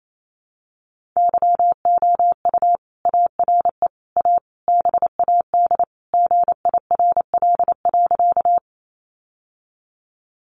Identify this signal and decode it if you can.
Morse code, heard as a keyed tone, about 28 wpm, 700 Hz. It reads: YOU ARE A BAD GIRL.
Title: Sentences from Top 700 Words 28wpm